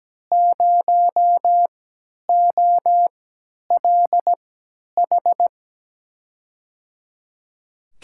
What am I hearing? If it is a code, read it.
Morse code, heard as a keyed tone, about 17 wpm, spaced as for 10 wpm, 700 Hz. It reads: 0OLH
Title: USA Call Signs - 6 Characters - 3x Character Spacing - 17wpm